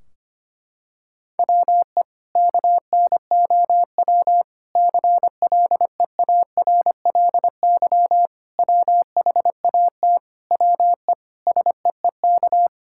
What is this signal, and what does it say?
Morse code, heard as a keyed tone, about 25 wpm, 700 Hz. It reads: WE KNOW CLEARLY WHAT WE SEEK
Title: Sentences from Top 1600 Words 25wpm